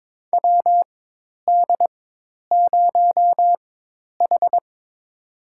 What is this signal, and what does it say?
Morse code, heard as a keyed tone, about 22 wpm, spaced as for 10 wpm, 700 Hz. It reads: WD0H
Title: USA Call Signs - 4 Characters - 4x Character Spacing - 22wpm